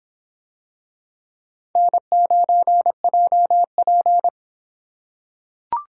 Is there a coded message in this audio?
Yes: Morse code from 1.8 to 4.3 s, heard as a keyed tone, about 25 wpm, 700 Hz. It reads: N9JP